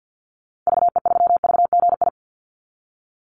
Morse code, heard as a keyed tone, about 50 wpm, 700 Hz. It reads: VE3VGI
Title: Call Signs Repeated 2x - 50wpm